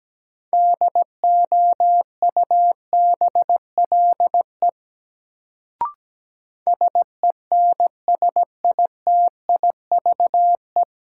Morse code, heard as a keyed tone, about 17 wpm, 700 Hz. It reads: DOUBLE SENSITIVE
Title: Sentences from Top 1400 Words - New Word Drill Alt - 17wpm